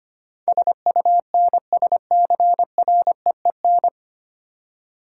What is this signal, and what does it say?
Morse code, heard as a keyed tone, about 25 wpm, 690 Hz. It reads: SUNSCREEN
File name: practice / 25wpm / Easy Words with Hints - 25wpm